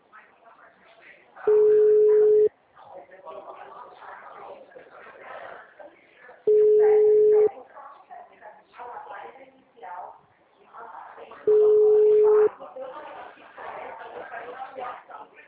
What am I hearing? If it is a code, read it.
Ringback tone (Europe, 425 Hz), with speech.